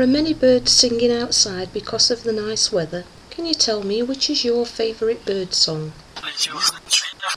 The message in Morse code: none